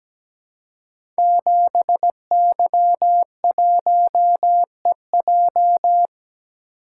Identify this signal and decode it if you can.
Morse code, heard as a keyed tone, about 17 wpm, 700 Hz. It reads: 7Y1EJ
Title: Code Groups - 5 Letter:Numbers - 17wpm